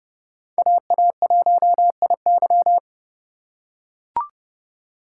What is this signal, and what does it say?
Morse code, heard as a keyed tone, about 29 wpm, 700 Hz. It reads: AA1IY